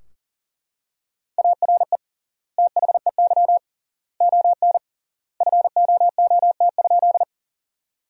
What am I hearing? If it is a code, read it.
Morse code, heard as a keyed tone, about 37 wpm, 700 Hz. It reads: ARE THEY ON FOOT?
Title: 3x Word-Spacing - Sentences from Top 500 Words - 40wpm